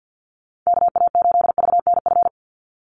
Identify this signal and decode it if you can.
Morse code, heard as a keyed tone, about 50 wpm, 700 Hz. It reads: KA8VNF
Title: USA Call Signs 50wpm